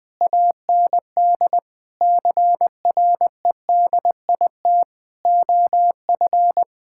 Morse code, heard as a keyed tone, about 21 wpm, 700 Hz. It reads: AND CREDIT OF
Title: Sentences with 8-Word Phrases - Spoken Beginning - 20wpm